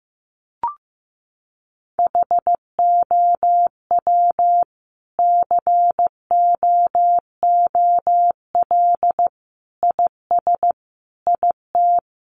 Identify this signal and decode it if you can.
Morse code, heard as a keyed tone, about 15 wpm, 700 Hz. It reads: HOW COOL IS IT
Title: Sentences from Top 1200 Words 15wpm